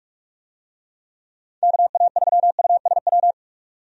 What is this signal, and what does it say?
Morse code, heard as a keyed tone, about 46 wpm, 700 Hz. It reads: KA3USW
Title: USA Call Signs - 6 Characters - 45wpm